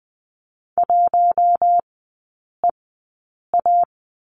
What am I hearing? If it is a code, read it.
Morse code, heard as a keyed tone, about 20 wpm, spaced as for 8 wpm, 700 Hz. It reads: 1EA
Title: 060 Lesson 11e - Letter h - Three Characters 20wpm